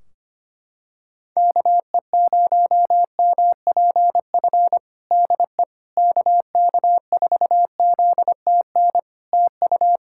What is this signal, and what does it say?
Morse code, heard as a keyed tone, about 25 wpm, 700 Hz. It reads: KE0MPF DE KK4ZTN TU
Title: Practice QSO - Exchange 4 - Ending - 25wpm